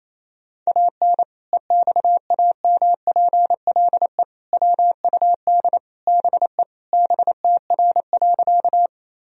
Morse code, heard as a keyed tone, about 27 wpm, 700 Hz. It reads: AN EXAMPLE WUD BE BTR.
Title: Sentences from Top 400 Words and Abbreviations 28wpm